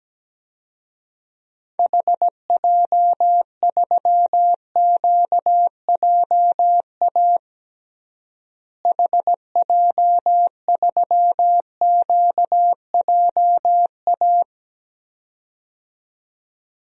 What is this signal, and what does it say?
Morse code, heard as a keyed tone, about 17 wpm, 700 Hz. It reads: HJ3QJA HJ3QJA
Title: DXCC-100 Call Signs - 6 Characters - Repeated 2x - 17wpm